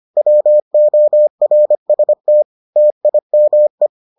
Morse code, heard as a keyed tone, about 25 wpm, 600 Hz. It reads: WORST TIME